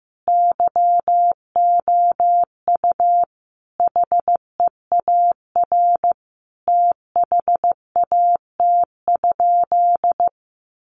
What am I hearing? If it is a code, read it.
Morse code, heard as a keyed tone, about 15 wpm, 700 Hz. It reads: YOU HEAR THAT?
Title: Sentences from Top 1500 Words 15wpm